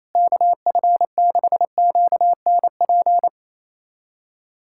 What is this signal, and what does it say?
Morse code, heard as a keyed tone, about 27 wpm, 700 Hz. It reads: KF6QNP
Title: USA Callsign DE Callsign - 6 Characters - 28wpm